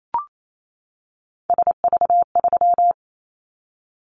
Morse code, heard as a keyed tone, about 28 wpm, 700 Hz. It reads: SV3
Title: Callsign Prefixes for Top 50 DXCC - 28wpm